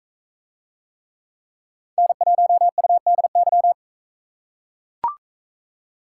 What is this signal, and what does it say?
Morse code, heard as a keyed tone, about 40 wpm, 700 Hz. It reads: N1UDY